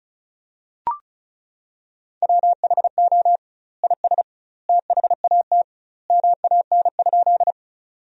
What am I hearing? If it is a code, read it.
Morse code, heard as a keyed tone, about 36 wpm, 700 Hz. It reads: WHO IS THAT MAN?
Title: 2x Word-Spacing - Sentences from Top 200 Words - 35wpm